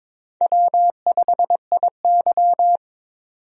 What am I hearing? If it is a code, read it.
Morse code, heard as a keyed tone, about 22 wpm, 700 Hz. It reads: W5IY
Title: USA Call Signs - 4 Characters - 22wpm